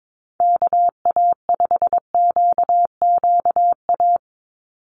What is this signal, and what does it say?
Morse code, heard as a keyed tone, about 22 wpm, 700 Hz. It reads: KA5QQA
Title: NA SOTA Chaser Callsigns 22wpm